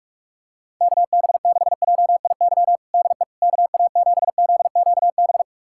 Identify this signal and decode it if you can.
Morse code, heard as a keyed tone, about 46 wpm, 700 Hz. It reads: KB6JIY DE KA7ZQB
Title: USA Callsign DE Callsign - 6 Characters - 45wpm